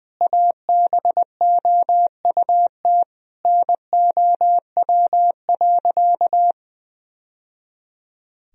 Morse code, heard as a keyed tone, about 20 wpm, 700 Hz. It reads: ABOUT NOW.
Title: Sentences from Top 1000 Words - Review Part 1 - 20wpm